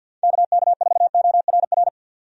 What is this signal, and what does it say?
Morse code, heard as a keyed tone, about 52 wpm, 700 Hz. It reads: KK4ORR
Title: USA Call Signs - 6 Characters - 50wpm